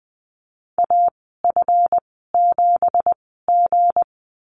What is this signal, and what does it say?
Morse code, heard as a keyed tone, about 20 wpm, spaced as for 15 wpm, 700 Hz. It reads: AF7G
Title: 213 Lesson 34g - Number 8 - Call Signs 20wpm